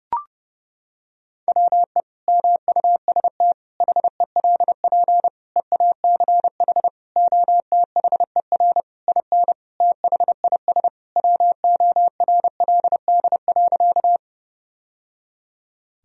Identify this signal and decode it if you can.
Morse code, heard as a keyed tone, about 30 wpm, 700 Hz. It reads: WE MUST HELP EACH OTHER IN THIS WORLD.
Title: Sentences from Top 1000 Words - Review Part 1 - 30wpm